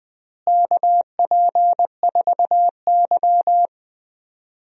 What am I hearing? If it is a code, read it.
Morse code, heard as a keyed tone, about 20 wpm, 700 Hz. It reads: KP4Y
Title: DXCC-100 Call Signs - 4 Characters - Repeated 2x - 20wpm